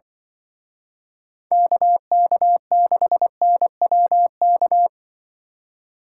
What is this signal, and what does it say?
Morse code, heard as a keyed tone, about 25 wpm, 700 Hz. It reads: KK6NWK
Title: Speed Racing - USA 2x3 Call Signs 20wpm